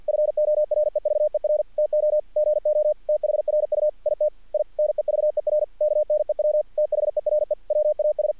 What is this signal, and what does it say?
Morse code, heard as a keyed tone, about 50 wpm, 600 Hz. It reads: FOREVER TO GO THRU IT I NEVER KNEW THERE WAS
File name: archived / CRSnet